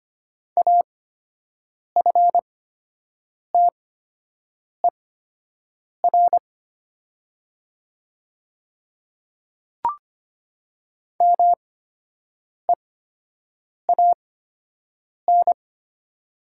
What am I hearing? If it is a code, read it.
Morse code, heard as a keyed tone, about 25 wpm, spaced as for 7 wpm, 690 Hz. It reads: AFTER MEAN
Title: 8x Character Spacing - Top 200 Words - 25wpm